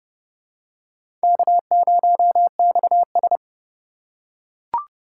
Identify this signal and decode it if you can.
Morse code, heard as a keyed tone, about 27 wpm, 700 Hz. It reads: K0XS